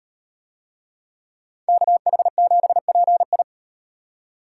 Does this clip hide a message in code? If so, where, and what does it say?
Yes: Morse code from 1.7 to 3.4 s, heard as a keyed tone, about 36 wpm, 700 Hz. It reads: KH7PI